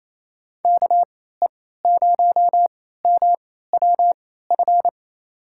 Morse code, heard as a keyed tone, about 28 wpm, spaced as for 16 wpm, 690 Hz. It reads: KE0MWF
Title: USA Call Signs - 3x Character Spacing - 28wpm